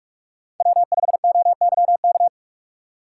Morse code, heard as a keyed tone, about 44 wpm, 690 Hz. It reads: W5OYK
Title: USA Call Signs - 6x Character Spacing - 45wpm